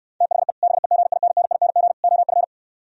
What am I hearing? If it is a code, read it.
Morse code, heard as a keyed tone, about 67 wpm, 700 Hz. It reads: THE BEGINNING OF